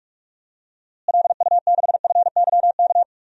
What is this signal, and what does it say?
Morse code, heard as a keyed tone, about 42 wpm, 700 Hz. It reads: LU6FYK
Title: DXCC-100 Call Signs - 6 Characters - Repeated 3x - 45wpm